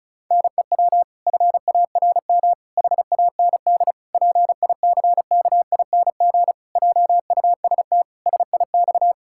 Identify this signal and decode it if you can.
Morse code, heard as a keyed tone, about 34 wpm, 700 Hz. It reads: NEW FARM HAND PICKING JUST SIX
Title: Pangram Challenge v2 - 35wpm